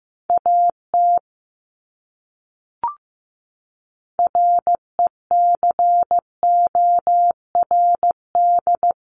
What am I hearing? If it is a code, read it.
Morse code, heard as a keyed tone, about 15 wpm, 700 Hz. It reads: AT RECORD
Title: Top 500 Words Repeated 15wpm